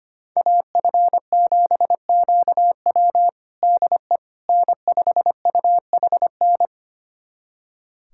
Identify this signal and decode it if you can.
Morse code, heard as a keyed tone, about 25 wpm, 700 Hz. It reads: AF7QW DE N5UHN
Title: USA Callsign DE Callsign - 5 Characters - 25wpm